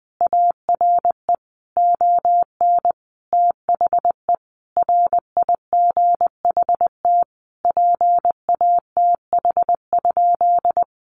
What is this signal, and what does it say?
Morse code, heard as a keyed tone, about 20 wpm, 700 Hz. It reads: ARE ON THE RIGHT PATH?
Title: Sentences from Top 1400 Words 20wpm